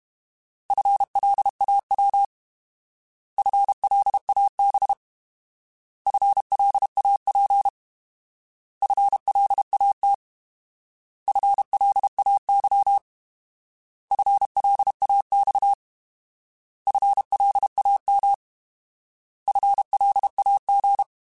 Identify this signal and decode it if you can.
Morse code, heard as a keyed tone, about 32 wpm, 800 Hz. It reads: FLAW FLAB FLAP FLAT FLAY FLAX FLAM FLAG
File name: Alphabetical / Alphabetical mp3s / Fs @30